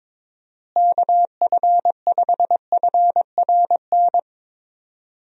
Morse code, heard as a keyed tone, about 22 wpm, 700 Hz. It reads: KF5FRN
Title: USA Call Signs - 6 Characters - Repeated 2x - 22wpm